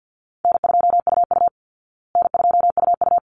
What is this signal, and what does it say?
Morse code, heard as a keyed tone, about 48 wpm, 700 Hz. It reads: N2UU N2UU
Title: Call Signs Repeated 3x - 50wpm